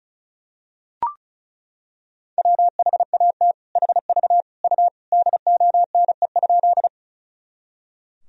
Morse code, heard as a keyed tone, about 34 wpm, 700 Hz. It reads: WHAT HV U DONE?
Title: Sentences from Top 500 Words and Abbreviations 35wpm